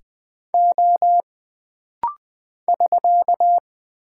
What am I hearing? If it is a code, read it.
Morse code, heard as a keyed tone, about 20 wpm, 700 Hz. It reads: O <SK>
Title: Single Character (Letter:Number:Punctuation:Prosign) - Rapid-Fire - 20wpm